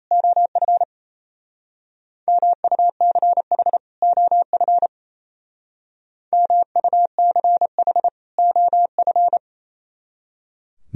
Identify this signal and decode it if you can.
Morse code, heard as a keyed tone, about 34 wpm, 700 Hz. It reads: OF MUCH OF MUCH OF
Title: Speed-Racing - Sets of 2 Words 28wpm